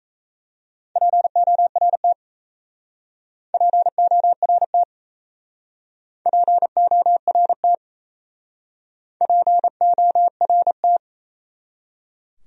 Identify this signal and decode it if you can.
Morse code, heard as a keyed tone, about 34 wpm, 700 Hz. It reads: PORT PORT PORT PORT
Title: Speed-Racing Top 300 Words 28wpm